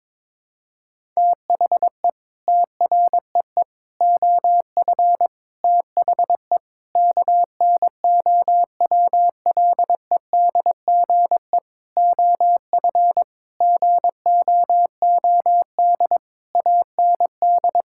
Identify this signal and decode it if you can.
Morse code, heard as a keyed tone, about 22 wpm, 700 Hz. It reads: THE TREE OF THE KNOWLEDGE OF GOOD AND
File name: Sets of 9 Words - 22wpm